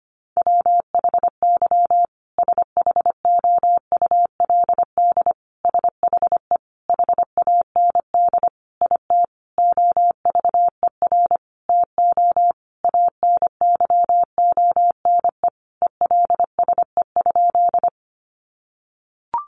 Morse code, heard as a keyed tone, about 25 wpm, 700 Hz. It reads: WHY SHOULD SHE HAND IT OVER TO ANYONE ELSE?
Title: Sentences from Top 900 Words 25wpm